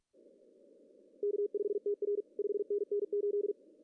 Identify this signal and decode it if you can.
Morse code, heard as a keyed tone, about 48 wpm, 410 Hz. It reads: K5TR 5NN 8
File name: CW / cqww-cw-k5tr